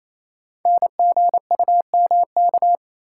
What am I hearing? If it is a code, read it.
Morse code, heard as a keyed tone, about 27 wpm, 700 Hz. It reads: NGUMK
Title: Code Groups - 5 Letters - 28wpm